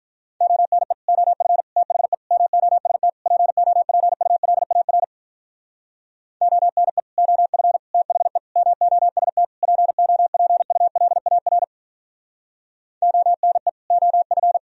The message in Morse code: ONE OF THE MOST POPULAR ONE OF THE MOST POPULAR ONE OF

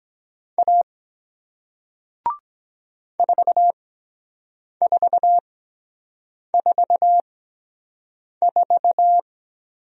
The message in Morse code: A 4444

